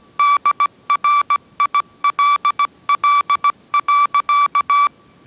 Morse code, heard as a keyed tone, about 17 wpm, 1160 Hz. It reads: DRILL.